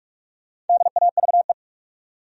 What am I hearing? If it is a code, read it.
Morse code, heard as a keyed tone, about 46 wpm, 700 Hz. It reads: DAVE
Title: USA Common Names - Max 4 Letters - 45wpm